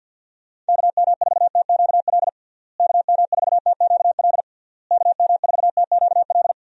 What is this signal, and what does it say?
Morse code, heard as a keyed tone, about 51 wpm, 700 Hz. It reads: KM4TQL KM4TQL KM4TQL
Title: USA Call Signs - 6 Characters - Repeated 3x - 50wpm